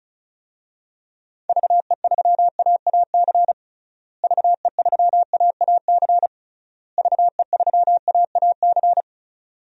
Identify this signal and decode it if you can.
Morse code, heard as a keyed tone, about 36 wpm, 700 Hz. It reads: VE3AAC VE3AAC VE3AAC
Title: DXCC-100 Call Signs - 6 Characters - Repeated 3x - 35wpm